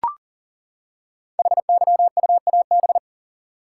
Morse code, heard as a keyed tone, about 40 wpm, 700 Hz. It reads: HYUAB
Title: Code Groups - 5 Letter:Numbers - 40wpm